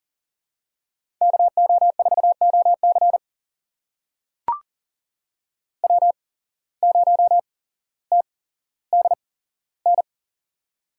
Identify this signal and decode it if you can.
Morse code, heard as a keyed tone, about 39 wpm, spaced as for 17 wpm, 700 Hz. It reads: KO4OC W0TDN
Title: USA Call Signs - 5 Characters - 8x Character Spacing - 40wpm